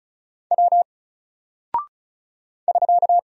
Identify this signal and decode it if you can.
Morse code, heard as a keyed tone, about 34 wpm, 700 Hz. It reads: W <SK>
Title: Single Character (Letter:Number:Punctuation:Prosign) 35wpm